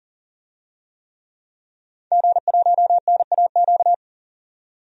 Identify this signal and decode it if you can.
Morse code, heard as a keyed tone, about 40 wpm, 700 Hz. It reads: G1NAQ
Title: DXCC-100 Call Signs - 5 Characters - 40wpm